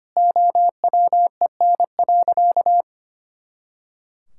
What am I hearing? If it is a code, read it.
Morse code, heard as a keyed tone, about 25 wpm, 700 Hz. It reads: OWEN.
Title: Pangram Challenge v2 - 25wpm